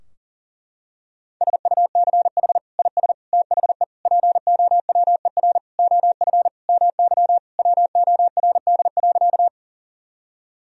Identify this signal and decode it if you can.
Morse code, heard as a keyed tone, about 40 wpm, 700 Hz. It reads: SUCH IS THE POWER OF MY WORD.
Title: Sentences from Top 1000 Words - Review Part 2 - 40wpm